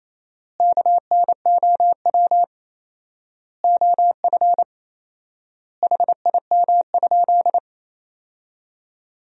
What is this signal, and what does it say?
Morse code, heard as a keyed tone, about 27 wpm, 700 Hz. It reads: KNOW OF HIM?
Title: 4x Word-Spacing - Sentences from Top 100 Words - 28wpm